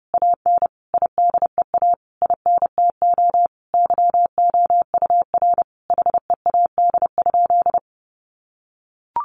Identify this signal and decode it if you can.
Morse code, heard as a keyed tone, about 30 wpm, 700 Hz. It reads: AN IDEA INTO YOUR HEAD?
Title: Sentences from Top 600 Words 30wpm